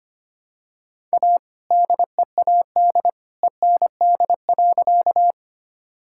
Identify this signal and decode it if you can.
Morse code, heard as a keyed tone, about 25 wpm, 700 Hz. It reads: A DEAD END.
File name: practice / 25wpm / All-Age Appropriate Jokes - 25wpm